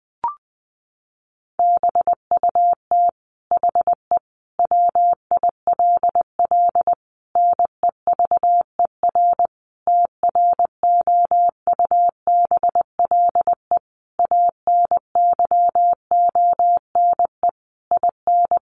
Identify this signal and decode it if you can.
Morse code, heard as a keyed tone, about 20 wpm, 700 Hz. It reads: BUT HE WILL NEVER TROUBLE ANYONE IN
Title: Sentences from Top 1200 Words 20wpm